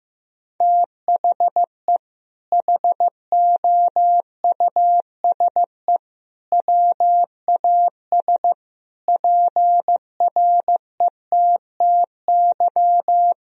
Morse code, heard as a keyed tone, about 15 wpm, 700 Hz. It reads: THE HOUSE WAS PRETTY
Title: Sentences from Top 1000 Words - Review Part 3 - 15wpm